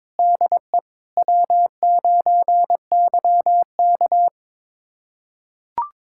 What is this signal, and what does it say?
Morse code, heard as a keyed tone, about 22 wpm, 700 Hz. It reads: DE W9YK
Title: USA Callsign DE Callsign - 4 Characters - 22wpm